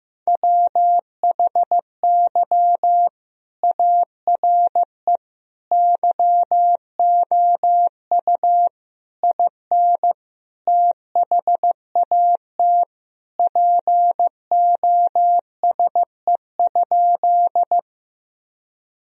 Morse code, heard as a keyed tone, about 15 wpm, 700 Hz. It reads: WHY ARE YOU IN THAT POSE?
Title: Sentences from Top 1000 Words - Review Part 2 - 15wpm